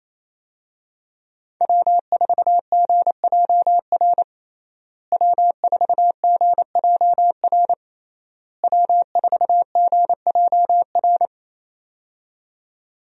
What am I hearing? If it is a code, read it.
Morse code, heard as a keyed tone, about 28 wpm, 700 Hz. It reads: W4GJR W4GJR W4GJR
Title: USA Call Signs - 5 Characters - Repeated 3x - 28wpm